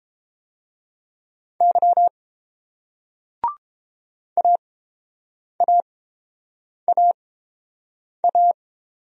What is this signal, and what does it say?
Morse code, heard as a keyed tone, about 30 wpm, spaced as for 7 wpm, 700 Hz. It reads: Y AAAA